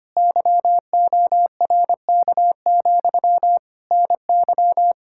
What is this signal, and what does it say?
Morse code, heard as a keyed tone, about 25 wpm, 700 Hz. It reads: YORK, NY
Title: Speed-Racing Common QSO Phrases - 25wpm